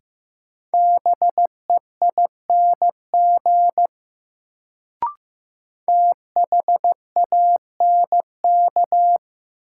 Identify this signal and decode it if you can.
Morse code, heard as a keyed tone, about 16 wpm, 700 Hz. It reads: BEING THANK
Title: Sentences from Top 600 Words - New Word Drill - 15wpm